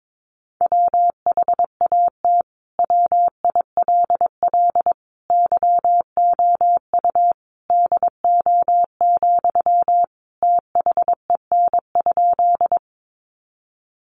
Morse code, heard as a keyed tone, about 22 wpm, 700 Hz. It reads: WHAT WILL YOU DO, THEN?
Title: Sentences from Top 100 Words 22wpm